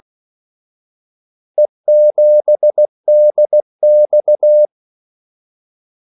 Morse code, heard as a keyed tone, about 16 wpm, 600 Hz. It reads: E7DX
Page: Kaldesignaler 16 wpm